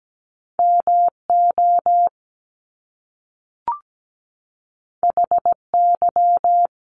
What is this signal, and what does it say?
Morse code, heard as a keyed tone, about 17 wpm, 700 Hz. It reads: MO HY
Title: Top Two Letter Combinations in Words 17wpm